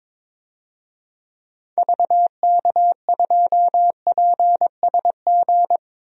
Morse code, heard as a keyed tone, about 22 wpm, 700 Hz. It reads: VK2PSG
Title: DXCC-100 Call Signs - 6 Characters - Repeated 2x - 22wpm